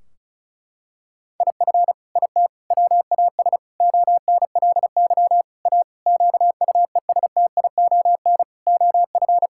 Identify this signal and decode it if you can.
Morse code, heard as a keyed tone, about 34 wpm, 700 Hz. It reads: IF IT WAS ONLY A QUESTION OF
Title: Sentences from Top 1600 Words 35wpm